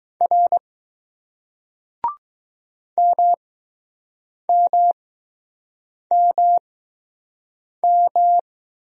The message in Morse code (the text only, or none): R MMMM